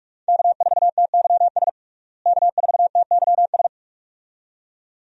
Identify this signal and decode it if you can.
Morse code, heard as a keyed tone, about 41 wpm, 700 Hz. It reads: K4TYS K4TYS
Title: USA Call Signs - 5 Characters - Repeated 2x - 45wpm